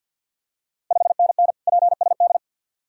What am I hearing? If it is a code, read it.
Morse code, heard as a keyed tone, about 50 wpm, 700 Hz. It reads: 5NN PSD
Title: Ontario QSO Party Exchange 50wpm